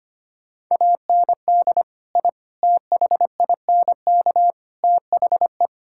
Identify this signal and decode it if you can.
Morse code, heard as a keyed tone, about 25 wpm, 700 Hz. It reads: AND I THINK THE
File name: practice / 25wpm / Sentences with 4-Word Phrases - 25wpm